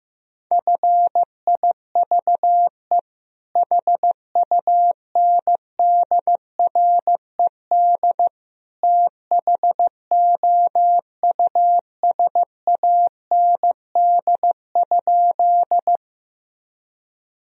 Morse code, heard as a keyed tone, about 16 wpm, 700 Hz. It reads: FIVE HUNDRED THOUSAND?
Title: Sentences from Top 1000 Words - Review Part 3 - 15wpm